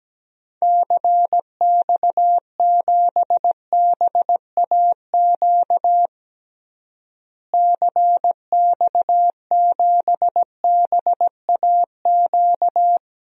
Morse code, heard as a keyed tone, about 17 wpm, 700 Hz. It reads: CX7BAQ CX7BAQ
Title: DXCC-100 Call Signs - 6 Characters - Repeated 2x - 17wpm